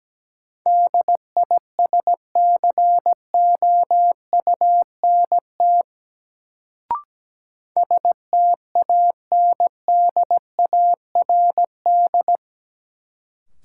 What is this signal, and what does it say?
Morse code, heard as a keyed tone, about 17 wpm, 700 Hz. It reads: DISCOUNT STANDARD
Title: Sentences from Top 1200 Words - New Word Drill Alt - 17wpm